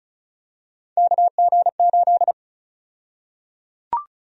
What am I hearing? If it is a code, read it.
Morse code, heard as a keyed tone, about 34 wpm, 700 Hz. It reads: KG8
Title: USA Call Sign Prefixes - Repeated 2x - 35wpm